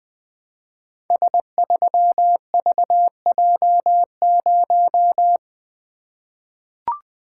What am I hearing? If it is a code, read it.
Morse code, heard as a keyed tone, about 20 wpm, 700 Hz. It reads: S3VJ0
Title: Code Groups - 5 Letter:Numbers - 20wpm